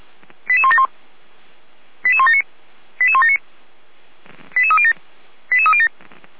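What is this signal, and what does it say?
Five-tone selective call (ZVEI), heard as quick runs of five tones, tones of about 80 ms. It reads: 88171 88179 88179 88307 88307
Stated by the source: ZVEI five-tone call received by SDR.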